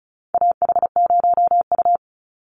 Morse code, heard as a keyed tone, about 34 wpm, 700 Hz. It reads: AH0U
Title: Call Signs - 35wpm